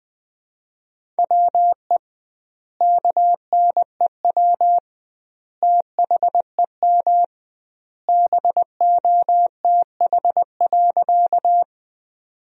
Morse code, heard as a keyed tone, about 20 wpm, 700 Hz. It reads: WE KNEW THEM BOTH.
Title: 2x Word-Spacing - Sentences from Top 400 Words - 20wpm